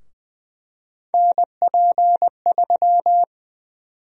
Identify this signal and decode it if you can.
Morse code, heard as a keyed tone, about 20 wpm, 700 Hz. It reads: NP3
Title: USA Call Sign Prefixes - Repeated 2x - 20wpm